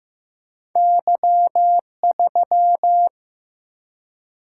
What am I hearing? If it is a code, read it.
Morse code, heard as a keyed tone, about 15 wpm, 700 Hz. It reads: Y3
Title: Single-Letter Single-Number 15wpm